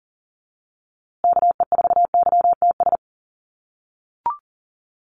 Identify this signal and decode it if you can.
Morse code, heard as a keyed tone, about 40 wpm, 700 Hz. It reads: KE4YTS
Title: USA Call Signs Repeated 3x 40wpm